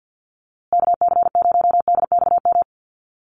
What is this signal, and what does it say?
Morse code, heard as a keyed tone, about 50 wpm, 700 Hz. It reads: KC0DXM
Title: USA Call Signs Repeated 2x 50wpm